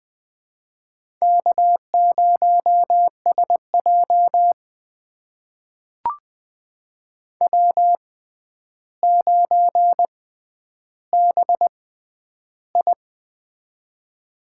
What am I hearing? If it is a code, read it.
Morse code, heard as a keyed tone, about 20 wpm, spaced as for 10 wpm, 700 Hz. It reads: K0SJ W9BI
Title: USA Call Signs - 4 Characters - 6x Character Spacing - 20wpm